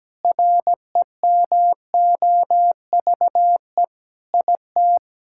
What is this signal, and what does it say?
Morse code, heard as a keyed tone, about 17 wpm, 700 Hz. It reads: REMOVE IT
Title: Sentences from Top 1000 Words - Review Part 5 - 17wpm